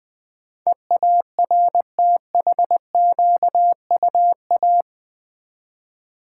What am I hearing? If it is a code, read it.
Morse code, heard as a keyed tone, about 20 wpm, 700 Hz. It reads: EARTHQUA
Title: Build A Word - 4 Letters At A Time - 10-Letter Words - 20wpm